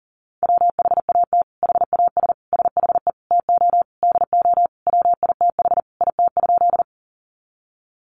Morse code, heard as a keyed tone, about 40 wpm, 700 Hz. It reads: WHAT HAS SHE TO DO WITH IT?
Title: Sentences from Top 100 Words 40wpm